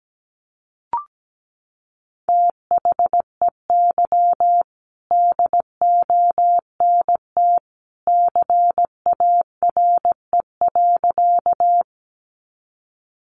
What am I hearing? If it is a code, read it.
Morse code, heard as a keyed tone, about 17 wpm, 700 Hz. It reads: THEY DONT CARE.
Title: Sentences from Top 300 Words 17wpm